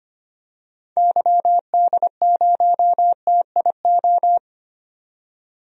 Morse code, heard as a keyed tone, about 25 wpm, 700 Hz. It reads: YD0TIO
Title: DXCC-100 Call Signs - 6 Characters - 25wpm